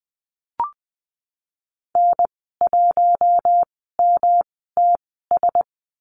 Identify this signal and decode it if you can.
Morse code, heard as a keyed tone, about 19 wpm, spaced as for 14 wpm, 700 Hz. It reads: N1MTS
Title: 122 Lesson 20g - Number 3 - Call Signs 20wpm